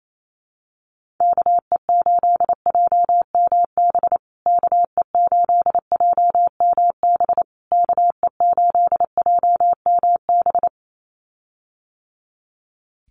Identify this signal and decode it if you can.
Morse code, heard as a keyed tone, about 28 wpm, 700 Hz. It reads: KE8JMB KE8JMB KE8JMB
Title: USA Call Signs Repeated 3x 28wpm